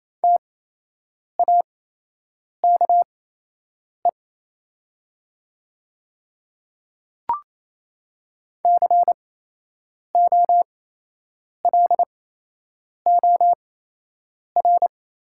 Morse code, heard as a keyed tone, about 28 wpm, spaced as for 8 wpm, 700 Hz. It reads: TAKE COLOR
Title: 8x Character Spacing - Top 300 Words - 28wpm